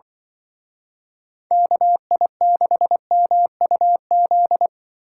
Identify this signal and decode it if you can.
Morse code, heard as a keyed tone, about 23 wpm, 700 Hz. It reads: KI6MUZ
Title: Speed Racing - USA Call Signs 20wpm